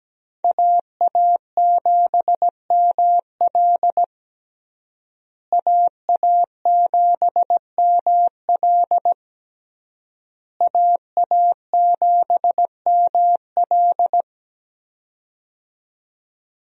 Morse code, heard as a keyed tone, about 17 wpm, 700 Hz. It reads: AA7ML AA7ML AA7ML
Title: USA Call Signs - 5 Characters - Repeated 3x - 17wpm